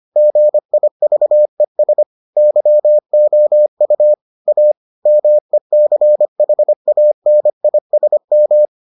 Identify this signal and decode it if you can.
Morse code, heard as a keyed tone, about 25 wpm, 600 Hz. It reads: GIVES YOU A MECHANISM